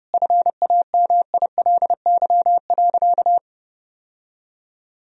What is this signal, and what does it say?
Morse code, heard as a keyed tone, about 32 wpm, 700 Hz. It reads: FAMILY.
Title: Sentences from Top 1000 Words - Review Part 4 - 30wpm